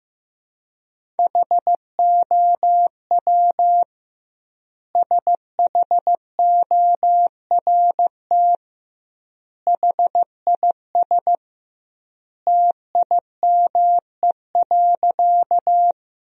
Morse code, heard as a keyed tone, about 15 wpm, 700 Hz. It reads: HOW SHORT HIS TIME.
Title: 2x Word-Spacing - Sentences from Top 400 Words - 15wpm